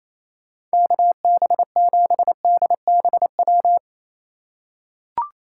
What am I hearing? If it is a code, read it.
Morse code, heard as a keyed tone, about 28 wpm, 700 Hz. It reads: KB7DBW
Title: USA Call Signs - 6 Characters - 2x Character Spacing - 28wpm